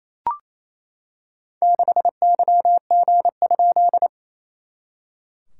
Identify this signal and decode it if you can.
Morse code, heard as a keyed tone, about 28 wpm, 690 Hz. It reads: 6YG?
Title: SOTA QSO Practice 28wpm